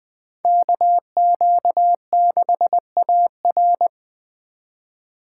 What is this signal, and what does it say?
Morse code, heard as a keyed tone, about 24 wpm, 700 Hz. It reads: KQ6AR
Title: Speed Racing - USA Call Signs 20wpm